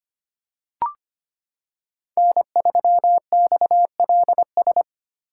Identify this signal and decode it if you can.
Morse code, heard as a keyed tone, about 25 wpm, 700 Hz. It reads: N3XLS
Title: K1USN SST QSO Practice - 25wpm